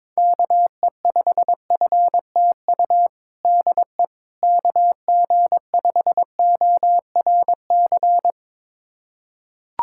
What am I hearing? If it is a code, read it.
Morse code, heard as a keyed tone, about 22 wpm, 700 Hz. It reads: KE5FTU DE KG5ORC
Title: USA Callsign DE Callsign 22wpm